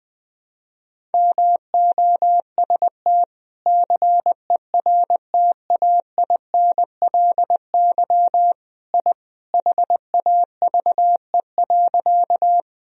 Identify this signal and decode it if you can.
Morse code, heard as a keyed tone, about 21 wpm, 700 Hz. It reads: MOST CERTAINLY I HAVE.
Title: Sentences from Top 1000 Words - Review Part 1 - 20wpm